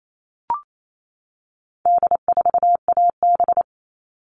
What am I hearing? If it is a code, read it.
Morse code, heard as a keyed tone, about 28 wpm, 700 Hz. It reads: D4AB
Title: All Cut Numbers - 4-Digit 28wpm